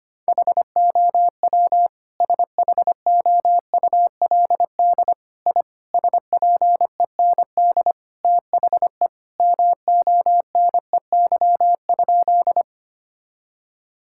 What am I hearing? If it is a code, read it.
Morse code, heard as a keyed tone, about 25 wpm, 700 Hz. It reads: HOW SHOULD I SPEND THE MONEY?
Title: Sentences from Top 1000 Words - Review Part 5 - 25wpm